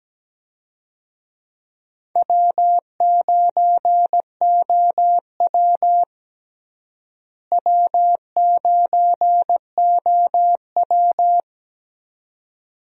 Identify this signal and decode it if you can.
Morse code, heard as a keyed tone, about 17 wpm, 700 Hz. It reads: W9OW W9OW
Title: USA Call Signs - 4 Characters - Repeated 2x - 17wpm